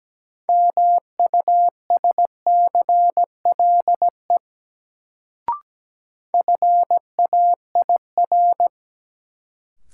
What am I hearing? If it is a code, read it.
Morse code, heard as a keyed tone, about 17 wpm, 700 Hz. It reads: MUSCLE FAIR
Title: Sentences from Top 1300 Words - New Word Drill Alt - 17wpm